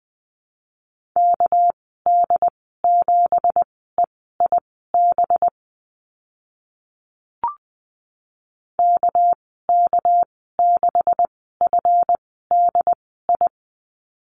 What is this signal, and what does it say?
Morse code, heard as a keyed tone, about 20 wpm, spaced as for 14 wpm, 700 Hz. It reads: KD7EIB KK6FDI
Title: 238 Lesson 38g - Letter q - Call Signs 20wpm